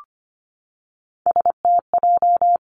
Morse code, heard as a keyed tone, about 25 wpm, 700 Hz. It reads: STJ
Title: Three Challenging Letter-Number Combinations 25wpm